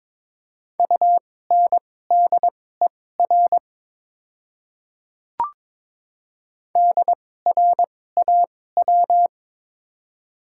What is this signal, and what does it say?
Morse code, heard as a keyed tone, about 22 wpm, spaced as for 16 wpm, 700 Hz. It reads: UNDER DRAW
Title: 2x Character Spacing - Top 500 Words - 22wpm